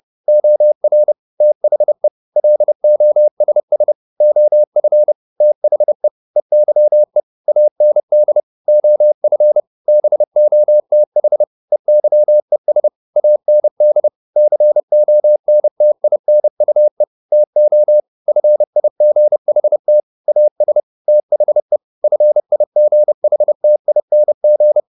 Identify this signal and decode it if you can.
Morse code, heard as a keyed tone, about 30 wpm, 600 Hz. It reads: OR THE LOSS OF THE EYE AND OF BOTH EYES AND CONTINUE TO FIGHT AS THE FIGHTING
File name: the old man and the sea / the old man and the sea 1 30wpm